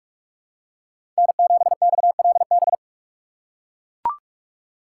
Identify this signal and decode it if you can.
Morse code, heard as a keyed tone, about 42 wpm, 700 Hz. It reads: N7XLB
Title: USA Call Signs - 5 Characters - 6x Character Spacing - 45wpm